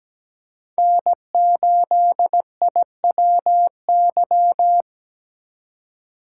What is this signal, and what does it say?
Morse code, heard as a keyed tone, about 17 wpm, 700 Hz. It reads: N8IWY
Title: K1USN SST QSO Practice - 17wpm